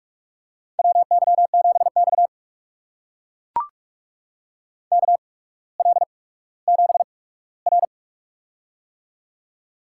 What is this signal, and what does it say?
Morse code, heard as a keyed tone, about 44 wpm, spaced as for 19 wpm, 690 Hz. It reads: WY7X KL7R
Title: USA Call Signs - 4 Characters - 8x Character Spacing - 45wpm